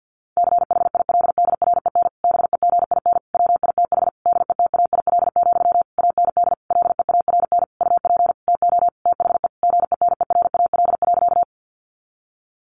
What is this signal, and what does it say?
Morse code, heard as a keyed tone, about 50 wpm, 700 Hz. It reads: CHILDREN BEGIN WITH DETAIL, AND LEARN UP TO THE GENERAL.
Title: Sentences from Top 1500 Words 50wpm